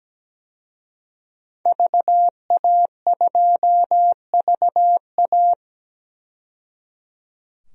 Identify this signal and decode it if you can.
Morse code, heard as a keyed tone, about 17 wpm, 700 Hz. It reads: VA2VA
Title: DXCC-100 Call Signs - 5 Characters - 17wpm